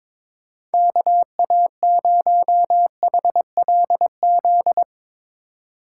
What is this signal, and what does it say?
Morse code, heard as a keyed tone, about 22 wpm, 700 Hz. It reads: KA0HLZ
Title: Speed Racing - USA 2x3 Call Signs 22wpm